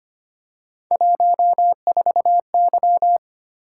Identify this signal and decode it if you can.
Morse code, heard as a keyed tone, about 29 wpm, 700 Hz. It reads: 14Y
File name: Speed-Racing - Three Letter-Number Combinations - 25wpm